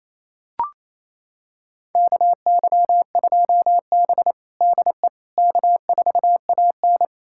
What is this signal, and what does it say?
Morse code, heard as a keyed tone, about 28 wpm, 700 Hz. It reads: KY2B DE K4AN
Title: USA Callsign DE Callsign - 4 Characters - 28wpm